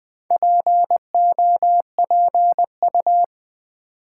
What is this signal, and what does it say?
Morse code, heard as a keyed tone, about 20 wpm, 700 Hz. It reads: POPU
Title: Build A Word - 4 Letters At A Time - 10-Letter Words - 20wpm